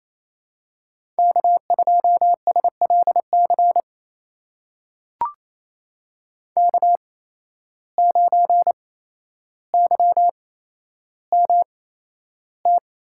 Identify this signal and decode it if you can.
Morse code, heard as a keyed tone, about 28 wpm, spaced as for 12 wpm, 700 Hz. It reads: K2SLC K9YMT
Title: USA Call Signs - 5 Characters - 8x Character Spacing - 28wpm